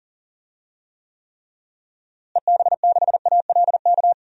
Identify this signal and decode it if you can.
Morse code, heard as a keyed tone, about 37 wpm, 700 Hz. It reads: EB6ALK
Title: DXCC-100 Call Signs - 6 Characters - Repeated 3x - 40wpm